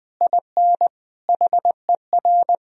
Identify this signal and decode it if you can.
Morse code, heard as a keyed tone, about 20 wpm, 700 Hz. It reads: IN HER
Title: Sentences from Top 1000 Words - Review Part 4 - 20wpm